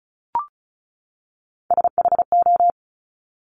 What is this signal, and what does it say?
Morse code, heard as a keyed tone, about 36 wpm, 700 Hz. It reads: SHO
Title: Top Three Letter Combinations in Words 35wpm